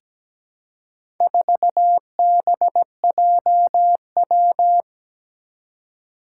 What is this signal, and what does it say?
Morse code, heard as a keyed tone, about 17 wpm, 700 Hz. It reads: 4BJW
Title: Four Challenging Letter-Number Combinations 17wpm